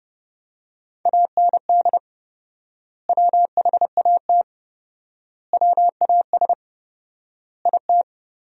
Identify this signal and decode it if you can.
Morse code, heard as a keyed tone, about 30 wpm, 700 Hz. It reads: AND WHAT WAS IT
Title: 4x Word-Spacing - Sentences from Top 100 Words - 30wpm